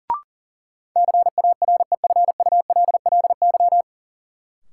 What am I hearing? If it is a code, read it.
Morse code, heard as a keyed tone, about 40 wpm, 700 Hz. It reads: CAREFULLY
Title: Sentences from Top 1300 Words - New Word Drill Alt - 40wpm